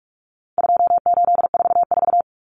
Morse code, heard as a keyed tone, about 44 wpm, 700 Hz. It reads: 2844
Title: Numbers - 4-Digit 45wpm